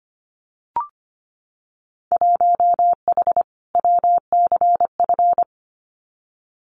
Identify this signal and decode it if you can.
Morse code, heard as a keyed tone, about 25 wpm, 700 Hz. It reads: 1H WCF
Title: Winter Field Day Exchanges 25wpm